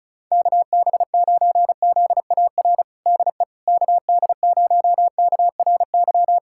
Morse code, heard as a keyed tone, about 34 wpm, 700 Hz. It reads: KB9ZAR DE KD0KRY
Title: USA Callsign DE Callsign - 6 Characters - 35wpm